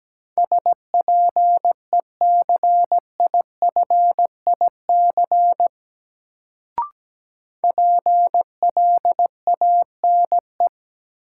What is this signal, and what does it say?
Morse code, heard as a keyed tone, about 17 wpm, 700 Hz. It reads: SPECIFIC PLANE
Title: Sentences from Top 1000 Words - Review Part 3 - New Word Drill - 17wpm